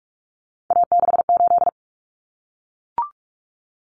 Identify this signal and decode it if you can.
Morse code, heard as a keyed tone, about 44 wpm, 700 Hz. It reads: A68
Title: Common Cut Numbers - 3-Digit 45wpm